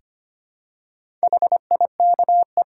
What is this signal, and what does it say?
Morse code, heard as a keyed tone, about 25 wpm, 700 Hz. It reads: HIKE
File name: practice / 25wpm / Easy Words with Hints - 25wpm